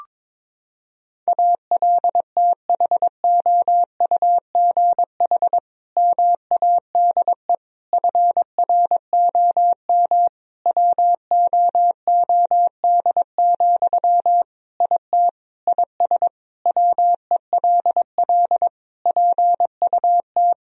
Morse code, heard as a keyed tone, about 22 wpm, 700 Hz. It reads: ALTHOUGH MADE FROM WOOD, IT IS WELL PUT
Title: Sentences from Top 1000 Words - Review Part 3 - 22wpm